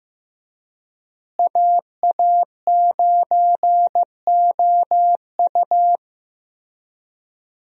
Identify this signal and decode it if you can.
Morse code, heard as a keyed tone, about 16 wpm, 700 Hz. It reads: AA9OU
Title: USA Call Signs - 5 Characters - Repeated 3x - 15wpm